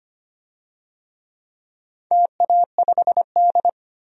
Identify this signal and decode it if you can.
Morse code, heard as a keyed tone, about 25 wpm, 700 Hz. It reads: TA5D
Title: DXCC-100 Call Signs - 4 Characters - 25wpm